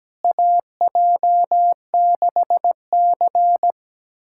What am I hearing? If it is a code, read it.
Morse code, heard as a keyed tone, about 17 wpm, 700 Hz. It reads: AJ6C
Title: USA Callsign DE Callsign - 4 Characters - 17wpm